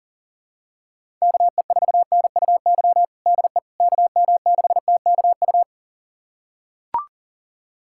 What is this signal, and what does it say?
Morse code, heard as a keyed tone, about 37 wpm, 700 Hz. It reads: KE4NUY DE KM6TKU
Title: USA Callsign DE Callsign 40wpm